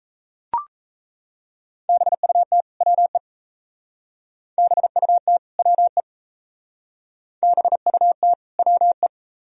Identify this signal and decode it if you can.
Morse code, heard as a keyed tone, about 34 wpm, 700 Hz. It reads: BUT WE BUT WE BUT WE
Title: Speed-Racing - Sets of 2 Words 28wpm